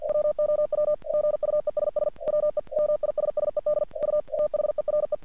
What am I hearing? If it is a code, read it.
Morse code, heard as a keyed tone, about 48 wpm, 600 Hz. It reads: WOW GUESS WE MISSED U THERE
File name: archived / CRSnet